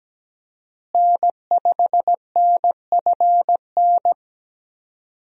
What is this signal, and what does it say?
Morse code, heard as a keyed tone, about 17 wpm, 700 Hz. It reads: N5NFN
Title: USA Call Signs - 5 Characters - Repeated 3x - 17wpm